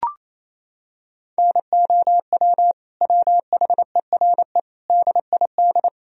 Morse code, heard as a keyed tone, about 28 wpm, 700 Hz. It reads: NOW WHERE DID
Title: Sentences from Top 200 Words and Abbreviations 28wpm